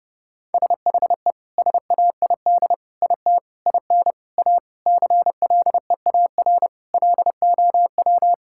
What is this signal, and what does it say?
Morse code, heard as a keyed tone, about 32 wpm, 700 Hz. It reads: SHE SAID IT IN A CLEAR LOW
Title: Sentences from Top 1600 Words 30wpm